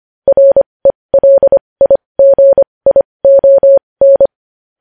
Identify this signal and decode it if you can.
Morse code, heard as a keyed tone, about 25 wpm, spaced as for 20 wpm, 550 Hz. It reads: RELIGION
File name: podcast / News headlines-1x-25x20